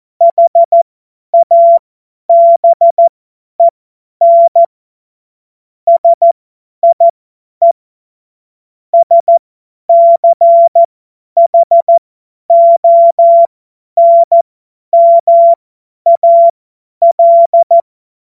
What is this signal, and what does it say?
Morse code, heard as a keyed tone, about 14 wpm, spaced as for 10 wpm, 680 Hz. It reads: HABEN SIE SCHONMAL
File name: Morsekurs / Witze / Witz 06. Januar 2026 50 F70